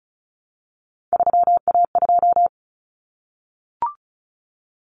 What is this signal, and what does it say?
Morse code, heard as a keyed tone, about 34 wpm, 700 Hz. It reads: 3A2
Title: Common Cut Numbers - 3-Digit 35wpm